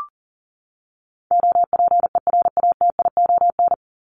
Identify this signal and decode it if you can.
Morse code, heard as a keyed tone, about 40 wpm, 700 Hz. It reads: OPERATION
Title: Top 1501-1750 Words 40wpm